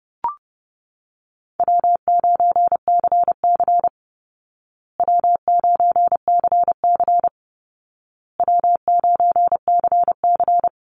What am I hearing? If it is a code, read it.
Morse code, heard as a keyed tone, about 30 wpm, 700 Hz. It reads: W9CC W9CC W9CC
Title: Easy Call Signs Repeated 3x - 30wpm